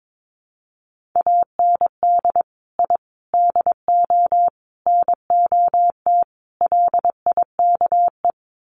Morse code, heard as a keyed tone, about 22 wpm, 700 Hz. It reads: AND I DO NOT LIKE
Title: Sets of 5 Words Encore2 22wpm